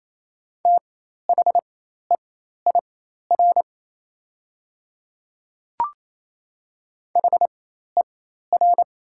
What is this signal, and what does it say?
Morse code, heard as a keyed tone, about 28 wpm, spaced as for 13 wpm, 690 Hz. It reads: THEIR HER
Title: 4x Character Spacing - Top 300 Words - 28wpm